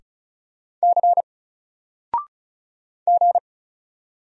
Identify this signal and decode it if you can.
Morse code, heard as a keyed tone, about 35 wpm, 700 Hz. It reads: C G
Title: Single Letter-Number 35wpm